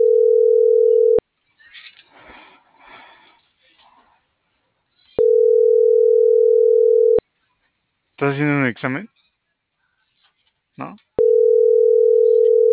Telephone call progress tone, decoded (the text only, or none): ringback tone